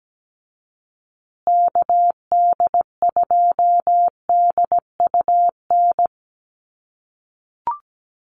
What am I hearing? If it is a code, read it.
Morse code, heard as a keyed tone, about 17 wpm, 700 Hz. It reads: KD2DUN
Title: USA Call Signs Repeated 2x 17wpm